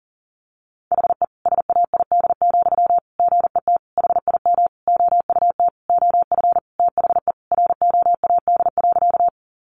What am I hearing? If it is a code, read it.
Morse code, heard as a keyed tone, about 40 wpm, 700 Hz. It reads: HE SAID, GET HIM OUT OF THE ROAD.
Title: Sentences from Top 400 Words 40wpm